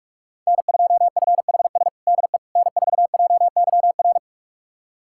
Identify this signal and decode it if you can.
Morse code, heard as a keyed tone, about 48 wpm, 700 Hz. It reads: N2FHS DE N4JYR
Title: USA Callsign DE Callsign - 5 Characters - 45wpm